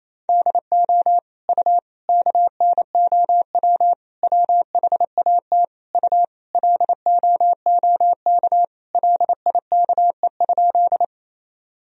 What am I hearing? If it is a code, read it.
Morse code, heard as a keyed tone, about 28 wpm, 700 Hz. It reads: DO U KNOW WHAT U LOOK LIKE?
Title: Sentences from Top 100 Words and Abbreviations 28wpm